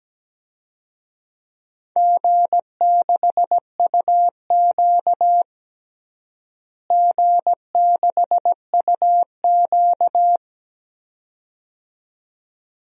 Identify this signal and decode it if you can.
Morse code, heard as a keyed tone, about 17 wpm, 700 Hz. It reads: G6UQ G6UQ
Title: DXCC-100 Call Signs - 4 Characters - Repeated 2x - 17wpm